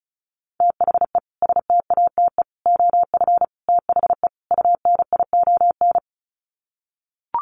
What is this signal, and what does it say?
Morse code, heard as a keyed tone, about 34 wpm, 700 Hz. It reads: THE STATE OF THE UNION
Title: Sets of 5 Words Encore 35wpm